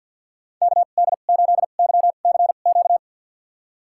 Morse code, heard as a keyed tone, about 52 wpm, spaced as for 36 wpm, 700 Hz. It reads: KD8YCQ
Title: USA Call Signs - 2x Character Spacing - 50wpm